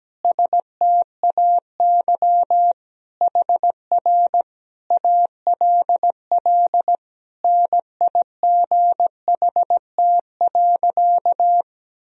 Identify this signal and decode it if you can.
Morse code, heard as a keyed tone, about 17 wpm, 700 Hz. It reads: STAY HR ALL NIGHT.
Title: Sentences from Top 500 Words and Abbreviations 17wpm